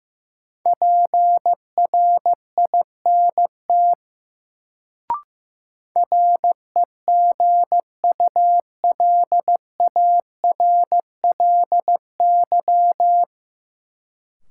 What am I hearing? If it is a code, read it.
Morse code, heard as a keyed tone, about 15 wpm, 700 Hz. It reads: PRINT REGULARLY
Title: Sentences from Top 1400 Words - New Word Drill - 15wpm